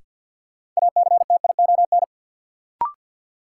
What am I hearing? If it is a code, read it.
Morse code, heard as a keyed tone, about 52 wpm, 700 Hz. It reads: ACTION
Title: Sentences from Top 700 Words - New Word Drill - 50wpm